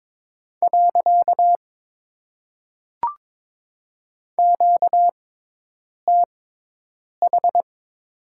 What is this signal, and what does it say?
Morse code, heard as a keyed tone, about 22 wpm, 700 Hz. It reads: . Q T H
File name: Top 90 QSO Elements - 6x Character Spacing - 22wpm